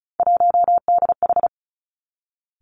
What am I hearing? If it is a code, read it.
Morse code, heard as a keyed tone, about 36 wpm, 700 Hz. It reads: 1DH
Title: Three Challenging Letter-Number Combinations 35wpm